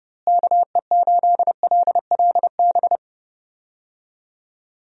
Morse code, heard as a keyed tone, about 30 wpm, 700 Hz. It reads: KE8LLB
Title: USA Call Signs - 6 Characters - Repeated 2x - 30wpm